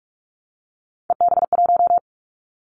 Morse code, heard as a keyed tone, about 46 wpm, 700 Hz. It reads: EB1
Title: Callsign Prefixes for Top 25 DXCC - 45wpm